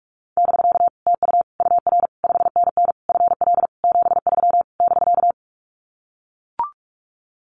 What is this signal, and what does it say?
Morse code, heard as a keyed tone, about 44 wpm, 700 Hz. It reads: <BK> TU UR 5NN FL 73 <BK>
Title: Common QSO Phrases Encore 45wpm